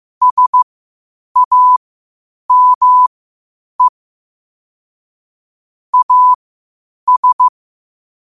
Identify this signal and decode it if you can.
Morse code, heard as a keyed tone, about 15 wpm, spaced as for 8 wpm, 1000 Hz. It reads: SAME AS
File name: qotdfiles / 8wpm